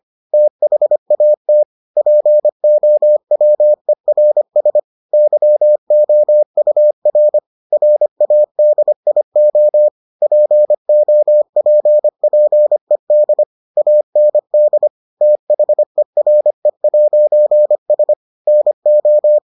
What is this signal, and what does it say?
Morse code, heard as a keyed tone, about 25 wpm, 600 Hz. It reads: THAT POWERS YOUR RADIO POPPED AND THERE'S NO